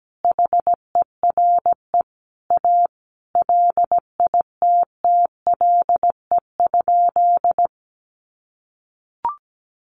Morse code, heard as a keyed tone, about 17 wpm, 700 Hz. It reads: HERE A LITTLE?
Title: Sentences from Top 500 Words 17wpm